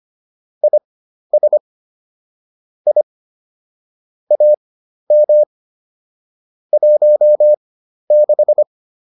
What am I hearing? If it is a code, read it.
Morse code, heard as a keyed tone, about 25 wpm, spaced as for 12 wpm, 600 Hz. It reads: IS I AM 16